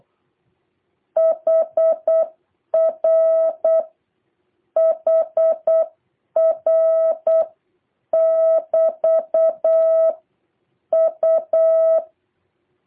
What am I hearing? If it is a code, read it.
Morse code, heard as a keyed tone, about 8 wpm, 650 Hz. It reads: HR HR <BT> U
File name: jh8cbh / 2023cw